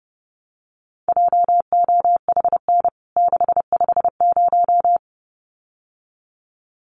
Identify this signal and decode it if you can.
Morse code, heard as a keyed tone, about 30 wpm, 700 Hz. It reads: JOHN 650
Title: CWT Exchange 30wpm